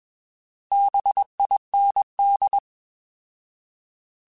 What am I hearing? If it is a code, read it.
Morse code, heard as a keyed tone, about 21 wpm, 800 Hz. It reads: BIND
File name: Bs @20